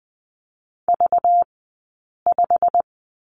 Morse code, heard as a keyed tone, about 20 wpm, spaced as for 8 wpm, 700 Hz. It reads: V5
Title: 211 Lesson 34d - Number 8 - Two Characters 20wpm